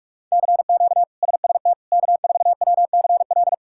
Morse code, heard as a keyed tone, about 46 wpm, 700 Hz. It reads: CQ SST K4WCL
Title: K1USN SST QSO Practice - 45wpm